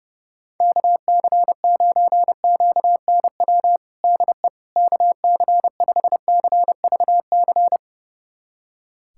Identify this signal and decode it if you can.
Morse code, heard as a keyed tone, about 30 wpm, 700 Hz. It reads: KC9QNW DE KC5CVC
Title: USA Callsign DE Callsign 30wpm